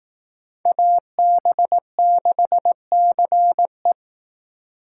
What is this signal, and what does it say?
Morse code, heard as a keyed tone, about 19 wpm, 700 Hz. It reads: AB6CE
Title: Speed Racing - USA 2x2 Call Signs 15wpm